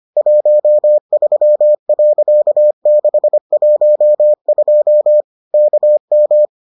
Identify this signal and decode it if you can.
Morse code, heard as a keyed tone, about 25 wpm, 600 Hz. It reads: 13.612 KM